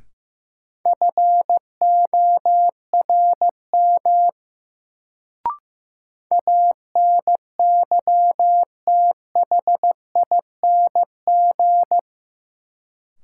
Morse code, heard as a keyed tone, about 15 wpm, 700 Hz. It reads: FORM ANYTHING
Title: Sets of 4 Words - New Word Drill - 15wpm